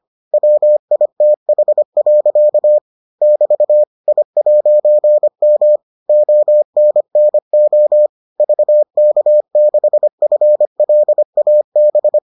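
Morse code, heard as a keyed tone, about 25 wpm, 600 Hz. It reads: WITH. <BT> I'M ONNO VK6FLAB
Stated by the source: Morse edition of the Foundations of Amateur Radio episodes.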